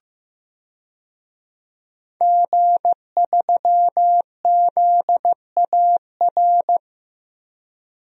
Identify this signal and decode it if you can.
Morse code, heard as a keyed tone, about 15 wpm, 700 Hz. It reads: G3ZAR
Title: DXCC-100 Call Signs - 5 Characters - 15wpm